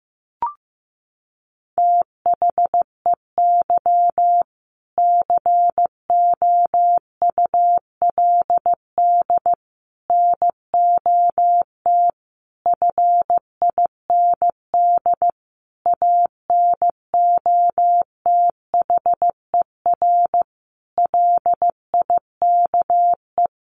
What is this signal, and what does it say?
Morse code, heard as a keyed tone, about 15 wpm, 700 Hz. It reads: THEY COULD NOT FIND ANOTHER LIKE
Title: Sentences from Top 1400 Words 15wpm